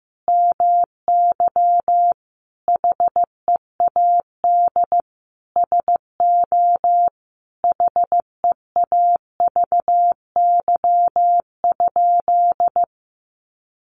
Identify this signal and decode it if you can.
Morse code, heard as a keyed tone, about 15 wpm, 700 Hz. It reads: MY HEAD SO HEAVY?
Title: Sentences from Top 1100 Words 15wpm